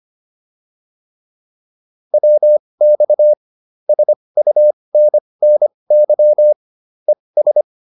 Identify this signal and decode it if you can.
Morse code, heard as a keyed tone, about 25 wpm, spaced as for 20 wpm, 600 Hz. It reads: WX SUNNY ES